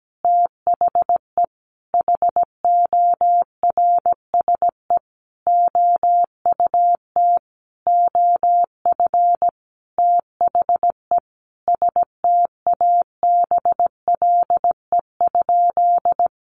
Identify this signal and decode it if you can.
Morse code, heard as a keyed tone, about 17 wpm, 700 Hz. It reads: THE HORSE OUT OF THE STABLE?
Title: Sentences from Top 1500 Words 17wpm